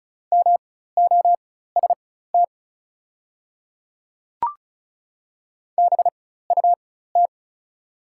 Morse code, heard as a keyed tone, about 36 wpm, spaced as for 16 wpm, 700 Hz. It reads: MOST BUT
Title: 4x Character Spacing - Top 200 Words - 35wpm